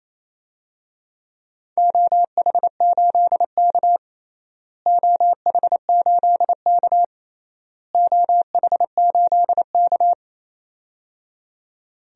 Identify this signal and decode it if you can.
Morse code, heard as a keyed tone, about 28 wpm, 700 Hz. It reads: OH8K OH8K OH8K
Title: DXCC-100 Call Signs - 4 Characters - Repeated 3x - 28wpm